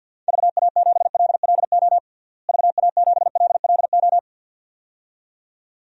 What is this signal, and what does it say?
Morse code, heard as a keyed tone, about 47 wpm, 700 Hz. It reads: VA7LLO VA7LLO
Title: DXCC-100 Call Signs - 6 Characters - Repeated 2x - 50wpm